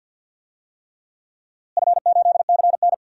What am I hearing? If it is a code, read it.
Morse code, heard as a keyed tone, about 52 wpm, 700 Hz. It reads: F8CN
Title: DXCC-100 Call Signs - 4 Characters - Repeated 3x - 50wpm